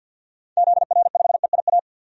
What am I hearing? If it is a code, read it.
Morse code, heard as a keyed tone, about 51 wpm, 700 Hz. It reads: ZR5EIA
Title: DXCC-100 Call Signs - 6 Characters - 50wpm